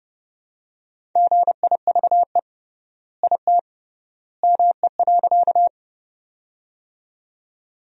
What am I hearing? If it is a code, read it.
Morse code, heard as a keyed tone, about 30 wpm, 700 Hz. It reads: GIVE IT ME.
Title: 3x Word-Spacing - Sentences from Top 200 Words - 30wpm